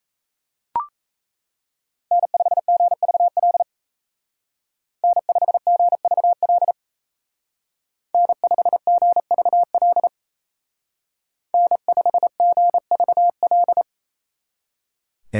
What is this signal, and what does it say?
Morse code, heard as a keyed tone, about 36 wpm, 700 Hz. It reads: N5GVL N5GVL N5GVL N5GVL